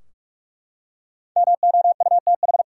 Morse code, heard as a keyed tone, about 42 wpm, 700 Hz. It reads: MOUTH
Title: Five Letter Words - 45wpm